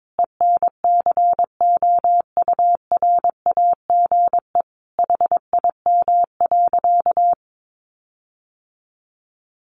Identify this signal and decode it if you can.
Morse code, heard as a keyed tone, about 22 wpm, 700 Hz. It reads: ENCOURAGE HIM.
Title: Sentences from Top 1400 Words 22wpm